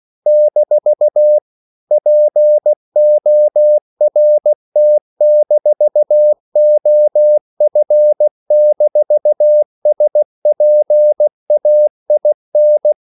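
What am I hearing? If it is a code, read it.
Morse code, heard as a keyed tone, about 16 wpm, 600 Hz. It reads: - PORT-OF-SPAIN
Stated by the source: CW practice of national capitals at 16 words per minute, from ON6ZQ.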